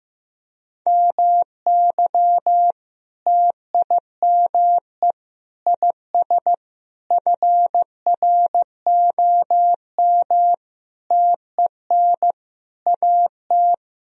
Morse code, heard as a keyed tone, about 16 wpm, 700 Hz. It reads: MY TIME IS FROM TEN AT
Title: Sentences from Top 1000 Words - Review Part 2 - 15wpm